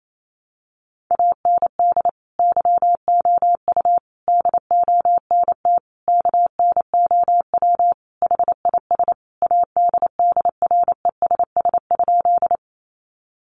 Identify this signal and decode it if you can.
Morse code, heard as a keyed tone, about 28 wpm, 700 Hz. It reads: AND YOU DONT KNOW HIS ADDRESS?
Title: Sentences from Top 1000 Words 28wpm